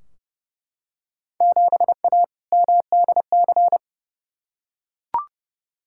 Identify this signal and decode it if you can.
Morse code, heard as a keyed tone, about 30 wpm, 700 Hz. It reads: 7A MDC
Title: Field Day Exchanges Encore - 30wpm